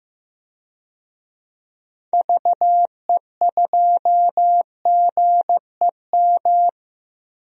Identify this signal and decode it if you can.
Morse code, heard as a keyed tone, about 15 wpm, 700 Hz. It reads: VE2GEM
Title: DXCC-100 Call Signs - 6 Characters - 15wpm